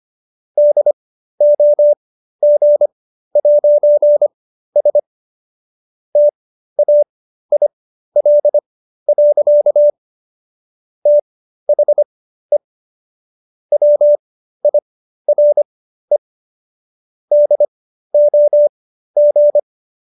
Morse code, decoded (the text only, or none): DOG'S TAIL. THE WIRE DOG